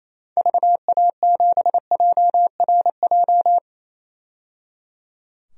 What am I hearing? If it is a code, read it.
Morse code, heard as a keyed tone, about 28 wpm, 700 Hz. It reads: VA7JRJ
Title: DXCC-100 Call Signs - 6 Characters - 28wpm